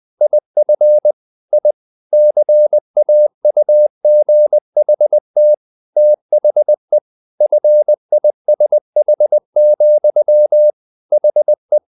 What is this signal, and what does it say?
Morse code, heard as a keyed tone, about 20 wpm, 600 Hz. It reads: IF I CAUGHT THE FISH, HE
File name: the old man and the sea part 2 20wpm